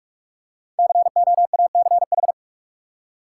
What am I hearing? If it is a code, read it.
Morse code, heard as a keyed tone, about 45 wpm, 700 Hz. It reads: COACH
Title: Five Letter Words - 45wpm